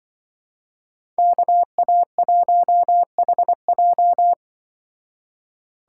Morse code, heard as a keyed tone, about 24 wpm, 700 Hz. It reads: KA1HJ
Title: Speed Racing - USA 2x2 Call Signs 20wpm